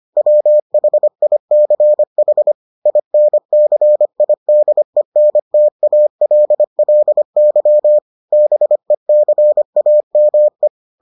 Morse code, heard as a keyed tone, about 25 wpm, 600 Hz. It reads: WHICH INCIDENTALLY BECAME